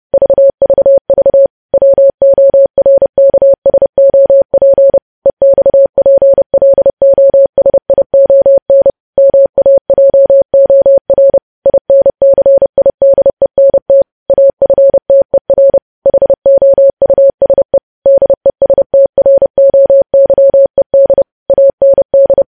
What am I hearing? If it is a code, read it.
Morse code, heard as a keyed tone, about 30 wpm, 550 Hz. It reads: VVV WORKSOP EXPLOSION MAJOR INCIDENT AFTER HOUSE DESTROYED AND
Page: news headlines in morse code at 30 wpm